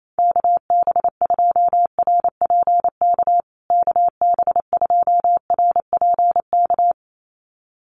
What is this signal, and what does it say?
Morse code, heard as a keyed tone, about 28 wpm, 700 Hz. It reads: KB2RPK KB2RPK
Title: USA Call Signs Repeated 2x 28wpm